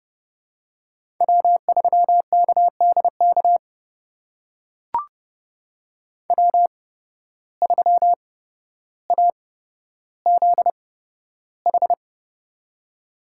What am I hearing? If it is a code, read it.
Morse code, heard as a keyed tone, about 30 wpm, spaced as for 13 wpm, 700 Hz. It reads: W3KDK W3AZH
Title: USA Call Signs - 5 Characters - 8x Character Spacing - 30wpm